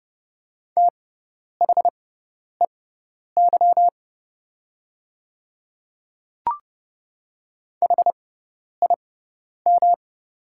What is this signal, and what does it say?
Morse code, heard as a keyed tone, about 30 wpm, spaced as for 10 wpm, 690 Hz. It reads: THEY HIM
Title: 6x Character Spacing - Top 200 Words - 30wpm